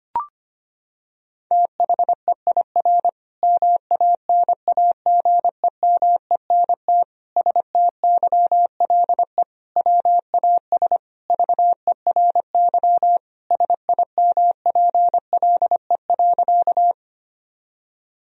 Morse code, heard as a keyed tone, about 25 wpm, 700 Hz. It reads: THEIR MANAGEMENT STYLE WAS VERY SIMPLE.
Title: Sentences from Top 1000 Words - Review Part 4 - 25wpm